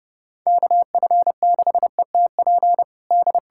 Morse code, heard as a keyed tone, about 32 wpm, 700 Hz. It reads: KF6ETP D
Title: USA Callsign DE Callsign - 6 Characters - 30wpm